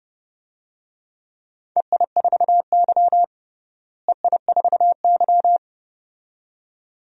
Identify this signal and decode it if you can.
Morse code, heard as a keyed tone, about 30 wpm, 700 Hz. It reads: EI4Y EI4Y
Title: DXCC-100 Call Signs - 4 Characters - Repeated 2x - 30wpm